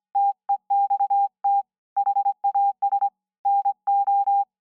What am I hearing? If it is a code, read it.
Morse code, heard as a keyed tone, about 25 wpm, 800 Hz. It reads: TEXT HAS NO